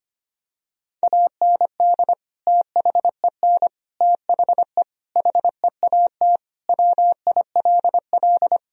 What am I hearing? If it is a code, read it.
Morse code, heard as a keyed tone, about 25 wpm, 700 Hz. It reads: AND THEN THE HEAT WILL
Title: Sentences from Top 1000 Words - Review Part 3 - 25wpm